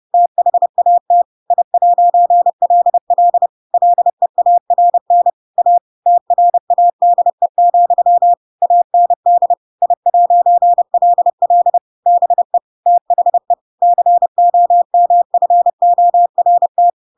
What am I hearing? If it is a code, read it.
Morse code, heard as a keyed tone, about 30 wpm, 700 Hz. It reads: THAT I'LL LEARN A TRADE, AND I'LL BE THE COMFORT
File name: Pinocchio / Pinocchio ch8 30 30 ik1hge